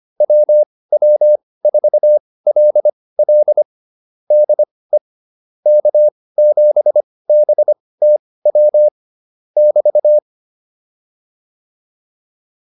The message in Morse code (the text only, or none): WW4LL DE K7BTW <BT>